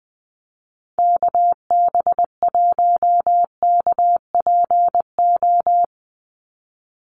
Morse code, heard as a keyed tone, about 20 wpm, 700 Hz. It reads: KB1KPO
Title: USA Call Signs 20wpm